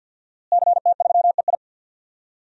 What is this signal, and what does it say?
Morse code, heard as a keyed tone, about 47 wpm, 700 Hz. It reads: CT3EI
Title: DXCC-100 Call Signs - 5 Characters - Repeated 2x - 50wpm